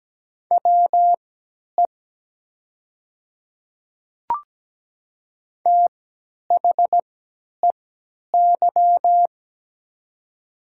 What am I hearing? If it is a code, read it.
Morse code, heard as a keyed tone, about 17 wpm, spaced as for 10 wpm, 700 Hz. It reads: WE THEY
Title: 3x Character Spacing - Top 100 Words - 17wpm